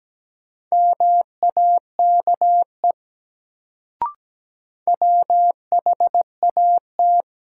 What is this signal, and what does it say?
Morse code, heard as a keyed tone, about 17 wpm, 700 Hz. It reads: MAKE WHAT
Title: Sets of 2 Words - New Word Drill - 17wpm